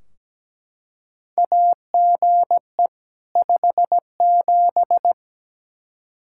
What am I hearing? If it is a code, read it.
Morse code, heard as a keyed tone, about 17 wpm, 700 Hz. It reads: AGE 57
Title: Practice QSO - Exchange 3 - More Info - 17wpm